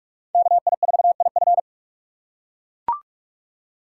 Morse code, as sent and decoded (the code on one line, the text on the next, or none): -.- .. ....- .. ..-.
KI4IF